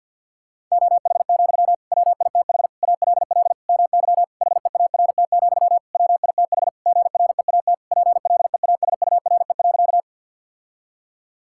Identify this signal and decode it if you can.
Morse code, heard as a keyed tone, about 51 wpm, 700 Hz. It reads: OH, WITH ALL MY HEART, WITH GREAT PLEASURE.
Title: Sentences from Top 1600 Words 50wpm